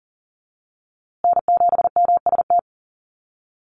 Morse code, heard as a keyed tone, about 40 wpm, 700 Hz. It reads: N7MST
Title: USA Call Signs Repeated 2x 40wpm